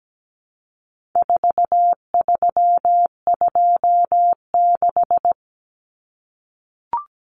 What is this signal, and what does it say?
Morse code, heard as a keyed tone, about 17 wpm, 700 Hz. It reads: 4326
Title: Common Cut Numbers - 4-Digit 17wpm